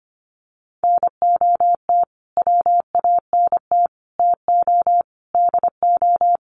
Morse code, heard as a keyed tone, about 25 wpm, 700 Hz. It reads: NOT WANT TO DO
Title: Sets of 5 Words Encore 25wpm